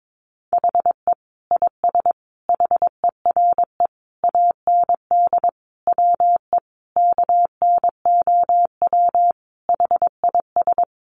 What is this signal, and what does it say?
Morse code, heard as a keyed tone, about 22 wpm, 700 Hz. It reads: HE IS HERE AND WE KNOW HIS
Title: Sentences from Top 800 Words 22wpm